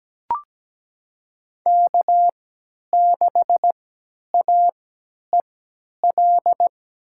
Morse code, heard as a keyed tone, about 17 wpm, spaced as for 10 wpm, 700 Hz. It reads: K6AEL
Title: USA Call Signs - 3x Character Spacing - 17wpm